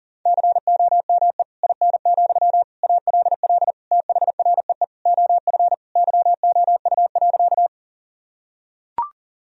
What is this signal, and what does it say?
Morse code, heard as a keyed tone, about 40 wpm, 700 Hz. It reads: COME IN, ALL THREE OF YOU.
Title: Sentences from Top 1000 Words - Review Part 1 - 40wpm